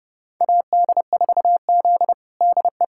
Morse code, heard as a keyed tone, about 32 wpm, 700 Hz. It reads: AD4Z DE
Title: USA Callsign DE Callsign - 4 Characters - 30wpm